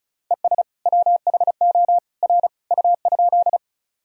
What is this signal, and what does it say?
Morse code, heard as a keyed tone, about 36 wpm, 700 Hz. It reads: ES WHO R U?
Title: Sentences from Top 100 Words and Abbreviations 35wpm